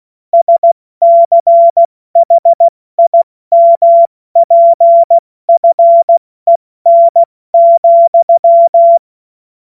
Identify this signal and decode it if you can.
Morse code, heard as a keyed tone, about 16 wpm, 680 Hz. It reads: SCHIMPFEN,
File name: Morsekurs / Witze / Witz 05 Dezember 2023 70 F80